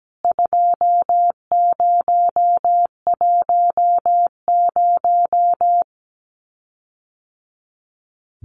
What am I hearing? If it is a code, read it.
Morse code, heard as a keyed tone, about 17 wpm, 700 Hz. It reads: 2010
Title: CWT Exchange 17wpm